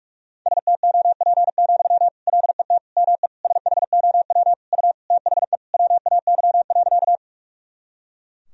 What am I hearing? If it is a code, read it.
Morse code, heard as a keyed tone, about 46 wpm, 700 Hz. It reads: STOP, LET ME SHOW U THE WAY.
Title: Sentences from Top 300 Words and Abbreviations 45wpm